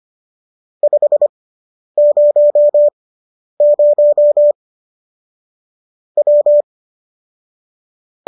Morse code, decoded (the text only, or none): 500 W